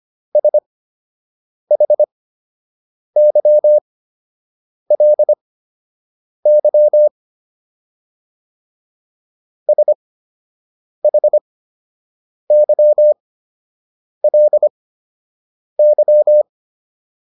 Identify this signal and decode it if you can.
Morse code, heard as a keyed tone, about 25 wpm, spaced as for 7 wpm, 600 Hz. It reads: SHYLY SHYLY